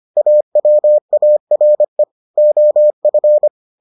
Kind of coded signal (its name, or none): Morse code